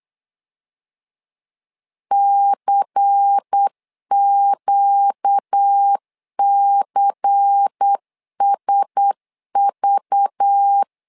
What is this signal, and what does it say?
Morse code, heard as a keyed tone, about 8 wpm, 800 Hz. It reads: CQCSV
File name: LEZIONI / 5 LBVC / 5 LBVC 8wpm 2935